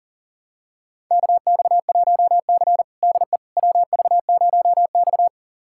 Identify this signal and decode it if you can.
Morse code, heard as a keyed tone, about 40 wpm, 700 Hz. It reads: KX1C DE WV0X
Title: USA Callsign DE Callsign - 4 Characters - 40wpm